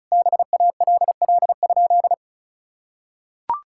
Morse code, heard as a keyed tone, about 36 wpm, 700 Hz. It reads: BALL?
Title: Sentences from Top 1000 Words - Review Part 6 - 35wpm